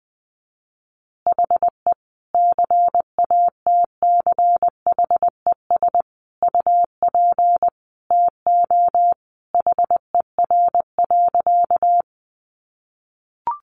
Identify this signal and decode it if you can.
Morse code, heard as a keyed tone, about 20 wpm, 700 Hz. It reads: HE CATCHES UP TO HER.
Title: Additional Easy Phrases 20wpm